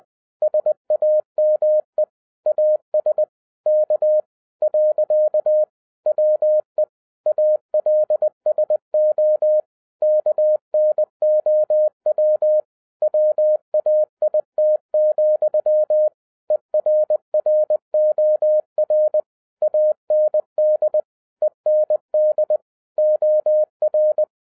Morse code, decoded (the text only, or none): SAME AS K. WE ALSO KNOW WAIT, ERROR AND END OR